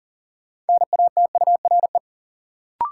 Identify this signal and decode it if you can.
Morse code, heard as a keyed tone, about 40 wpm, 700 Hz. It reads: NATURE
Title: Sentences from Top 600 Words - New Word Drill - 40wpm